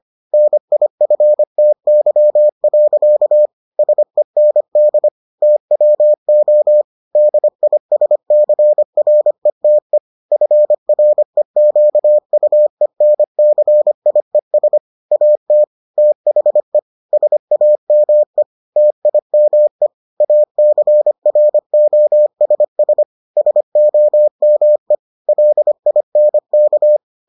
Morse code, heard as a keyed tone, about 25 wpm, 600 Hz. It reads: NIFTY. SEND TWO DISCRETE FREQUENCIES AT THE SAME TIME ACROSS SOME LINK